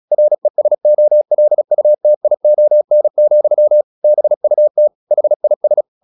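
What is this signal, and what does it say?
Morse code, heard as a keyed tone, about 36 wpm, 600 Hz. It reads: RESOLUTION, BUT HIS
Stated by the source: Morse practice at 35 wpm from CW med Gnister, sending Ken Follett's novel Never in English.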